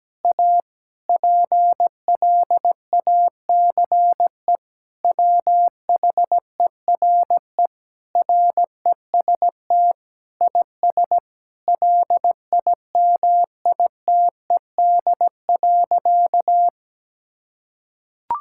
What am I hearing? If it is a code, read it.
Morse code, heard as a keyed tone, about 17 wpm, 700 Hz. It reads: A PLACE WHERE REST IS LIMITED.
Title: Sentences from Top 1000 Words - Review Part 6 - 17wpm